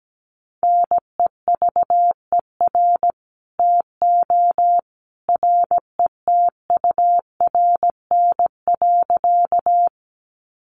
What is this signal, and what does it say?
Morse code, heard as a keyed tone, about 17 wpm, 700 Hz. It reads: NEVER TO RETURN.
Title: Sentences from Top 800 Words 17wpm